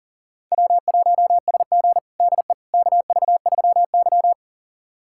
Morse code, heard as a keyed tone, about 40 wpm, 700 Hz. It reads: W1SG DE KV3Y
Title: USA Callsign DE Callsign - 4 Characters - 40wpm